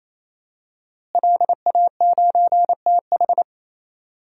Morse code, heard as a keyed tone, about 28 wpm, 700 Hz. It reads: LA9TH
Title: DXCC-100 Call Signs - 5 Characters - Repeated 3x - 28wpm